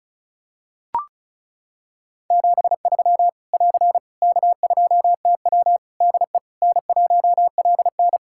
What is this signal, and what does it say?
Morse code, heard as a keyed tone, about 34 wpm, 700 Hz. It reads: 73 <AR> K2TW DE N1LN
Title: Practice QSO - Exchange 4 - Ending - 35wpm